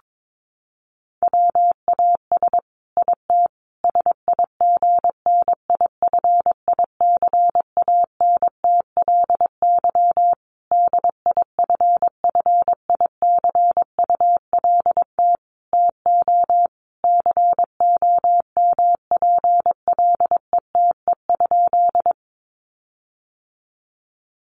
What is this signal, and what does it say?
Morse code, heard as a keyed tone, about 22 wpm, 700 Hz. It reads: WAS IT SIGNIFICANTLY DIFFICULT TO COMPLETE?
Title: Sentences from Top 1500 Words 22wpm